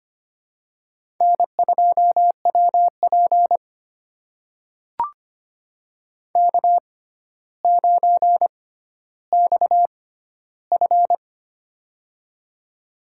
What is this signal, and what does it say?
Morse code, heard as a keyed tone, about 25 wpm, spaced as for 13 wpm, 700 Hz. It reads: N2WP K9XF
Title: USA Call Signs - 4 Characters - 6x Character Spacing - 25wpm